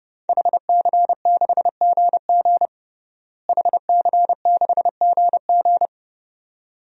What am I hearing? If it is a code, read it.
Morse code, heard as a keyed tone, about 30 wpm, 700 Hz. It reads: HC6GG HC6GG
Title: DXCC-100 Call Signs - 5 Characters - Repeated 2x - 30wpm